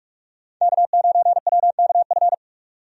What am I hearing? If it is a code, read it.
Morse code, heard as a keyed tone, about 42 wpm, 700 Hz. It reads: K9WKF
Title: USA Call Signs - 5 Characters - 45wpm